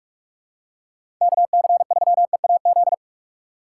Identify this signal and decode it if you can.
Morse code, heard as a keyed tone, about 44 wpm, 700 Hz. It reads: KC3EAZ
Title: USA Call Signs - 6 Characters - 2x Character Spacing - 45wpm